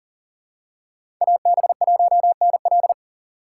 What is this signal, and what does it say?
Morse code, heard as a keyed tone, about 40 wpm, 700 Hz. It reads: AB1NL
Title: USA Call Signs - 5 Characters - 2x Character Spacing - 40wpm